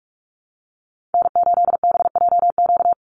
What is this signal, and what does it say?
Morse code, heard as a keyed tone, about 46 wpm, 700 Hz. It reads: N8BJQ
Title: Call Signs Repeated 3x - 45wpm